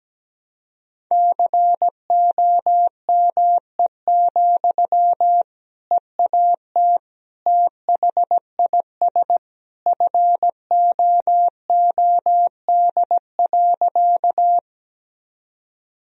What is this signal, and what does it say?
Morse code, heard as a keyed tone, about 17 wpm, 700 Hz. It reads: COME, EAT THIS FOOD.
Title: Sentences from Top 1000 Words - Review Part 1 - 17wpm